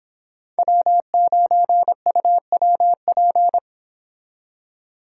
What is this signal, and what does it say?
Morse code, heard as a keyed tone, about 24 wpm, 700 Hz. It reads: W9UWP